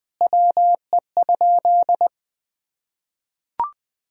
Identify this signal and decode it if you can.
Morse code, heard as a keyed tone, about 20 wpm, 700 Hz. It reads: WE?
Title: Sentences from Top 1000 Words - Review Part 3 - 20wpm